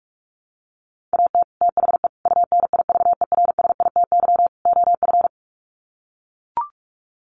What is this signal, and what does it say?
Morse code, heard as a keyed tone, about 44 wpm, 700 Hz. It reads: AT THE UNIVERSITY OF
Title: Sets of 4 Words 45wpm